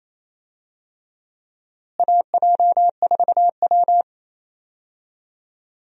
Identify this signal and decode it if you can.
Morse code, heard as a keyed tone, about 28 wpm, 700 Hz. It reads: AJ4W
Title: USA Call Signs - 4 Characters - Repeated 2x - 28wpm